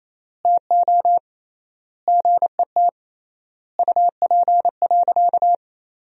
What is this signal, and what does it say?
Morse code, heard as a keyed tone, about 28 wpm, 700 Hz. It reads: TO GET UP.
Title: 3x Word-Spacing - Sentences from Top 500 Words - 28wpm